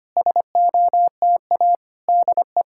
Morse code, heard as a keyed tone, about 25 wpm, 700 Hz. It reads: SOTA DE
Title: SOTA QSO Practice 25wpm